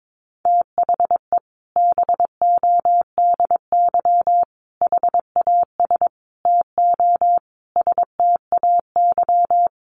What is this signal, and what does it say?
Morse code, heard as a keyed tone, about 22 wpm, 700 Hz. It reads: THE BODY HAS TO STAY
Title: A Sound Of Thunder 22wpm